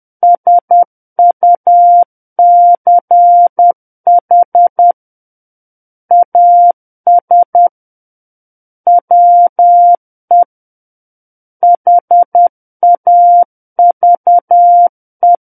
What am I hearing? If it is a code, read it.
Morse code, heard as a keyed tone, about 10 wpm, 700 Hz. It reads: SUCH AS WE HAVE
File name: Book One / 01-Chapter 1 - 10 WPM